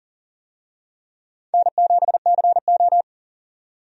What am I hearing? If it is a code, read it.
Morse code, heard as a keyed tone, about 40 wpm, 700 Hz. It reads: N7CO
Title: USA Call Signs - 4 Characters - 40wpm